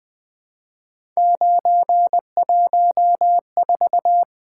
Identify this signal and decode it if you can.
Morse code, heard as a keyed tone, about 20 wpm, 700 Hz. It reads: 914
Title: Three Letter-Number Combinations - 20wpm